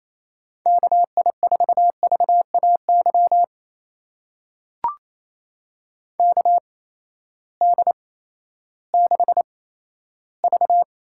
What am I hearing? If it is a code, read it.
Morse code, heard as a keyed tone, about 28 wpm, spaced as for 14 wpm, 700 Hz. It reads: KI4VAY KD6V
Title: USA Call Signs - 6 Characters - 8x Character Spacing - 28wpm